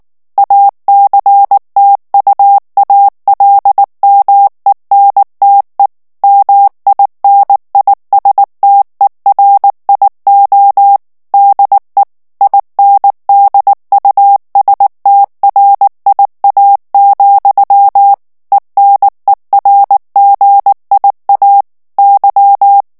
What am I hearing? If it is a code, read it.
Morse code, heard as a keyed tone, about 19 wpm, 800 Hz. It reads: ACTUALMENTE MINISTERIO DE INDUSTRIA, ENERGIA Y
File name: morse mp3 / ESPAGNOL TEXTE EN CLAIR 2 VITESSE 18 MOTS MINUTES